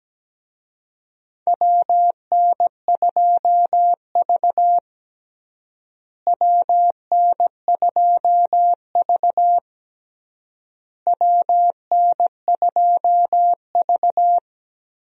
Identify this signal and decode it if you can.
Morse code, heard as a keyed tone, about 17 wpm, 700 Hz. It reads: WN2V WN2V WN2V
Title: USA Call Signs - 4 Characters - Repeated 3x - 17wpm